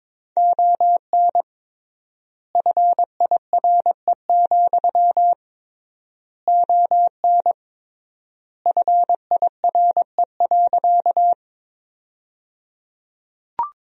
Morse code, heard as a keyed tone, about 22 wpm, 700 Hz. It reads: ON FIRE, ON FIRE.
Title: 3x Word-Spacing - Sentences from Top 400 Words - 22wpm